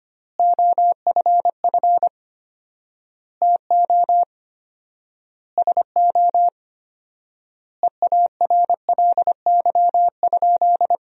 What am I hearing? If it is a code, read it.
Morse code, heard as a keyed tone, about 25 wpm, 700 Hz. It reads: OFF TO SO EARLY?
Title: 4x Word-Spacing - Sentences from Top 400 Words - 25wpm